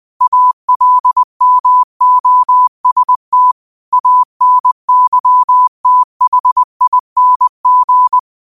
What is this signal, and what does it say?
Morse code, heard as a keyed tone, about 20 wpm, 1000 Hz. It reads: ALMOST ANYTHING
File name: qotdfiles / 20wpm